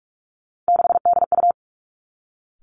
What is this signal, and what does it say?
Morse code, heard as a keyed tone, about 46 wpm, 700 Hz. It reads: 6DU
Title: Three Challenging Letter-Number Combinations 45wpm